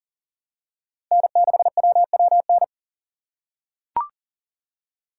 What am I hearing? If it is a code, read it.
Morse code, heard as a keyed tone, about 40 wpm, 700 Hz. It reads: N6WWN
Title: USA Call Signs - 6x Character Spacing - 40wpm